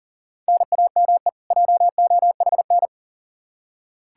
Morse code, heard as a keyed tone, about 40 wpm, 700 Hz. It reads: NAME JOHN
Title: Practice QSO - Exchange 1 - Beginning - 40wpm